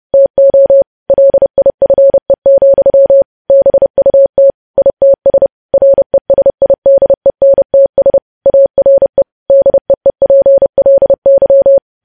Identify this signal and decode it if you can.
Morse code, heard as a keyed tone, about 30 wpm, 550 Hz. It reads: TO LIFE, BUT ITS RESIDENTS ARE DEEPLY
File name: podcast / News headlines-1x-30x30